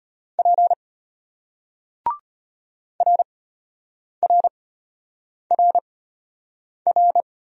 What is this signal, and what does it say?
Morse code, heard as a keyed tone, about 33 wpm, spaced as for 8 wpm, 700 Hz. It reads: P RRRR